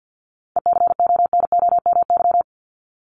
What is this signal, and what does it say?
Morse code, heard as a keyed tone, about 50 wpm, 700 Hz. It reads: ECONOMY
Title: Top 501-750 Words 50wpm